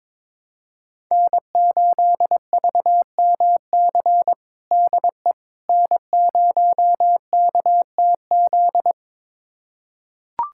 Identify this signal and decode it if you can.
Morse code, heard as a keyed tone, about 21 wpm, 700 Hz. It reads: N8VMC DE N0KTZ
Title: USA Callsign DE Callsign - 5 Characters - 22wpm